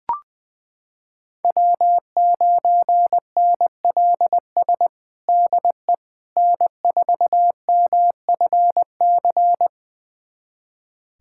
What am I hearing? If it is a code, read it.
Morse code, heard as a keyed tone, about 20 wpm, 700 Hz. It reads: W9NLS DE N4MFC
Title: USA Callsign DE Callsign 20wpm